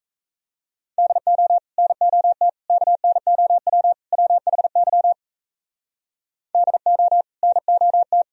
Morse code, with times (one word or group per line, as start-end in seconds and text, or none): 1.0-1.6 DO
1.8-2.5 NOT
2.7-3.9 KNOW
4.1-5.1 WHY
6.5-7.2 DO
7.4-8.2 NOT